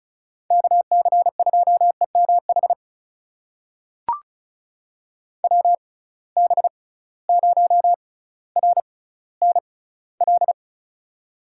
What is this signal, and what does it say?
Morse code, heard as a keyed tone, about 36 wpm, spaced as for 18 wpm, 700 Hz. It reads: KC2EMH WB0RNL
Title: USA Call Signs - 6 Characters - 6x Character Spacing - 35wpm